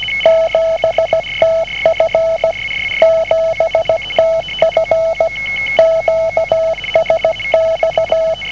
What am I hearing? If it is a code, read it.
Morse code, heard as a keyed tone, about 17 wpm, 650 Hz, with noise. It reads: 7TF 7TF QSX